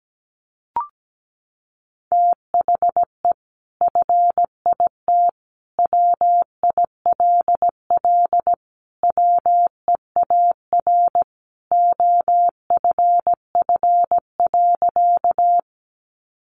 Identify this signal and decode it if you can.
Morse code, heard as a keyed tone, about 17 wpm, 700 Hz. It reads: THE FIT WILL WEAR OFF.
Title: Sentences from Top 1500 Words 17wpm